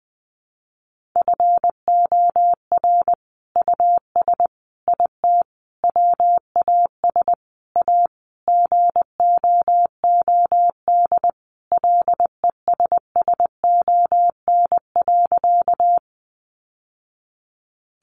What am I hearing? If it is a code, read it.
Morse code, heard as a keyed tone, about 20 wpm, 700 Hz. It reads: FOR US IT WAS A GOOD LESSON.
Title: Sentences from Top 1200 Words 20wpm